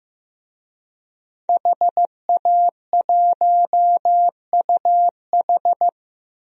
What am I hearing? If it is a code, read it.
Morse code, heard as a keyed tone, about 15 wpm, 700 Hz. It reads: HA1UH
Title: DXCC-100 Call Signs - 5 Characters - Repeated 3x - 15wpm